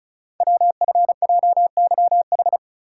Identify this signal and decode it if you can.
Morse code, heard as a keyed tone, about 36 wpm, 700 Hz. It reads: WFJYH
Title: Code Groups - 5 Letters - 35wpm